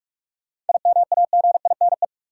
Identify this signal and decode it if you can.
Morse code, heard as a keyed tone, about 46 wpm, 700 Hz. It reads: IMAGINE
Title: Sentences from Top 1400 Words - New Word Drill Alt - 45wpm